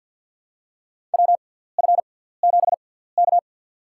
Morse code, heard as a keyed tone, about 52 wpm, spaced as for 17 wpm, 700 Hz. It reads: WF7K
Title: USA Call Signs - 4 Characters - 6x Character Spacing - 50wpm